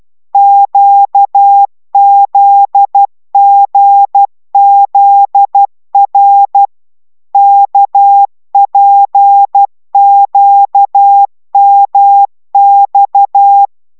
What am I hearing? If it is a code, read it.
Morse code, heard as a keyed tone, about 12 wpm, 800 Hz. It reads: QZGZR KPQMX